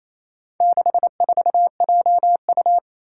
Morse code, heard as a keyed tone, about 28 wpm, 700 Hz. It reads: 64JU
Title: Four Challenging Letter-Number Combinations 28wpm